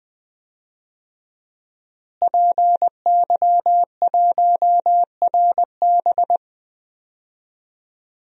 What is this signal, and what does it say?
Morse code, heard as a keyed tone, about 21 wpm, 700 Hz. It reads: PY1RB
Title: DXCC-100 Call Signs - 5 Characters - Repeated 2x - 20wpm